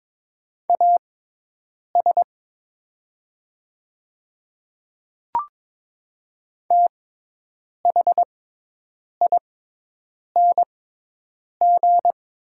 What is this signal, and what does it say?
Morse code, heard as a keyed tone, about 22 wpm, spaced as for 8 wpm, 700 Hz. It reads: AS THING
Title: 6x Character Spacing - Top 100 Words - 22wpm